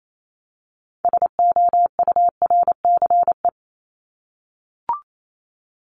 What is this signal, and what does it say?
Morse code, heard as a keyed tone, about 28 wpm, 700 Hz. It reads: SOURCE
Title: Top 501-750 Words 28wpm